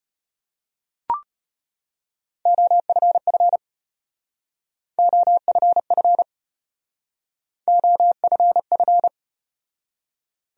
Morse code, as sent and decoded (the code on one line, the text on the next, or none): --- ..-. ..-. / --- ..-. ..-. / --- ..-. ..-.
OFF OFF OFF